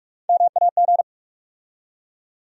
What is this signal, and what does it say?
Morse code, heard as a keyed tone, about 46 wpm, 700 Hz. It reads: MAG
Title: USA Call Sign Postfixes - 3 Letters 45wpm